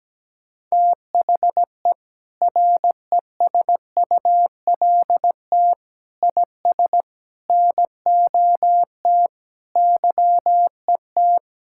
Morse code, heard as a keyed tone, about 17 wpm, 700 Hz. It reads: THE RESULT IS NOT YET
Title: Sentences from Top 1000 Words - Review Part 5 - 17wpm